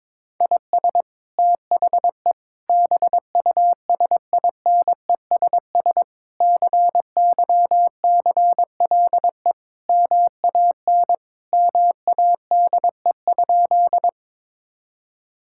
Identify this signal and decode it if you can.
Morse code, heard as a keyed tone, about 22 wpm, 700 Hz. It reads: IS THE BUSINESS CYCLE MAN MADE?
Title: Sentences from Top 1000 Words - Review Part 6 - 22wpm